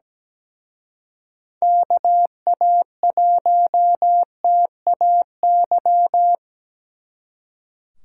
Morse code, heard as a keyed tone, about 21 wpm, 700 Hz. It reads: KA1TAY
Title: Speed Racing - USA 2x3 Call Signs 17wpm